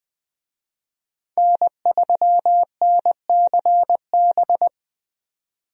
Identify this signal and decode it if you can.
Morse code, heard as a keyed tone, about 20 wpm, 700 Hz. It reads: N3NCB
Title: USA Call Signs - 5 Characters - 20wpm